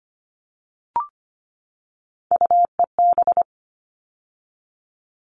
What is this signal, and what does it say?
Morse code, heard as a keyed tone, about 25 wpm, 700 Hz. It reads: UEB
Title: All Cut Numbers - 3-Digit 25wpm